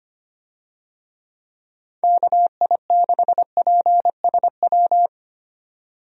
Morse code, heard as a keyed tone, about 25 wpm, 700 Hz. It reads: KI6PSW
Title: USA Call Signs - 6 Characters - 25wpm